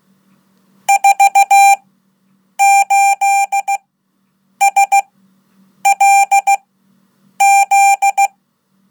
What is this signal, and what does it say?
Morse code, heard as a keyed tone, about 16 wpm, spaced as for 8 wpm, 790 Hz. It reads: 48SLZ
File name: Code / 5min-1